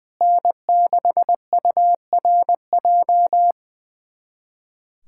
Morse code, heard as a keyed tone, about 20 wpm, 700 Hz. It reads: N6URJ
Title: USA Callsign DE Callsign - 5 Characters - 20wpm